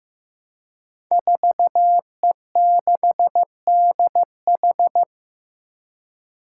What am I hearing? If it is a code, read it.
Morse code, heard as a keyed tone, about 15 wpm, 700 Hz. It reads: 4E6DH
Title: Five Challenging Letter-Number Combinations 15wpm